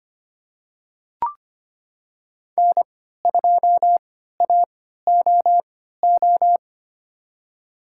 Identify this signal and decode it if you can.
Morse code, heard as a keyed tone, about 25 wpm, spaced as for 14 wpm, 700 Hz. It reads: N2AOO
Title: USA Call Signs - 5 Characters - 3x Character Spacing - 25wpm